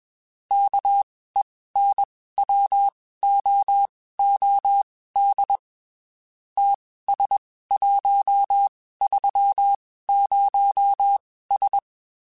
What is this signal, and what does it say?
Morse code, heard as a keyed tone, about 21 wpm, spaced as for 15 wpm, 800 Hz. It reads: KENWOOD TS130S
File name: (spaced) / QSOs Spaced-mp3s / WD0BGP 1 @20 sp